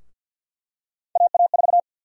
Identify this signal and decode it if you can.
Morse code, heard as a keyed tone, about 52 wpm, 700 Hz. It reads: AA4
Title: USA Call Sign Prefixes - 50wpm